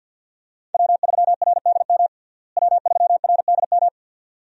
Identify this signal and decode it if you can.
Morse code, heard as a keyed tone, about 47 wpm, 700 Hz. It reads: W3RDM W3RDM
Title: USA Call Signs - 5 Characters - Repeated 2x - 50wpm